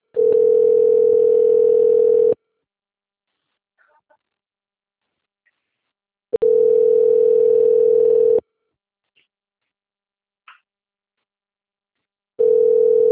Ringback tone (North America), 440 + 480 Hz.